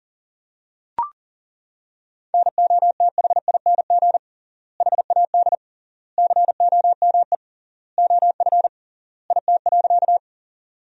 Morse code, heard as a keyed tone, about 40 wpm, 700 Hz. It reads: NOTHING HAD COME OF IT.
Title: 3x Word-Spacing - Sentences from Top 500 Words - 40wpm